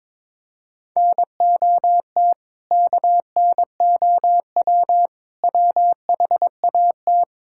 Morse code, heard as a keyed tone, about 22 wpm, 700 Hz. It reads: NOT KNOW WHAT
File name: practice / 22wpm / Build A Phrase - 5-Word Phrases - 22wpm